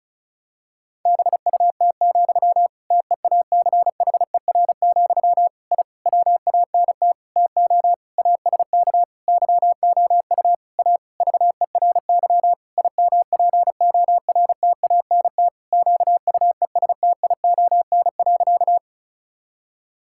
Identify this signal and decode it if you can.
Morse code, heard as a keyed tone, about 36 wpm, 700 Hz. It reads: BUT, TEACHER, I WANT TO ASK YOU A VERY IMPORTANT QUESTION.
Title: Sentences from Top 1000 Words - Review Part 6 - 35wpm